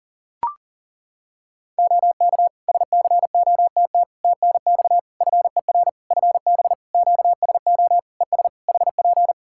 Morse code, heard as a keyed tone, about 40 wpm, 700 Hz. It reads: OK SCOTT TNX FER FB QSO ES HP
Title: Practice QSO - Exchange 4 - Ending - 40wpm